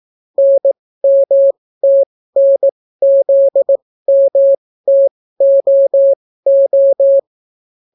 Morse code, heard as a keyed tone, about 18 wpm, spaced as for 14 wpm, 550 Hz. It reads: NMTNZMTOO